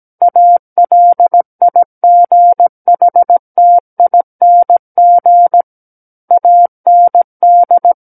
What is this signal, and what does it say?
Morse code, heard as a keyed tone, about 17 wpm, 700 Hz. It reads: ALIGHTING AND
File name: Book One / 08-Chapter 8 - 17 WPM